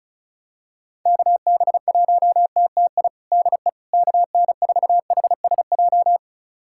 Morse code, heard as a keyed tone, about 36 wpm, 700 Hz. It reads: KB1TTI DE KN4HSJ
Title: USA Callsign DE Callsign 35wpm